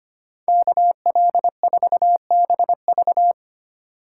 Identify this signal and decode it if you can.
Morse code, heard as a keyed tone, about 26 wpm, 700 Hz. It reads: KL4BV
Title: Speed Racing - USA 2x2 Call Signs 25wpm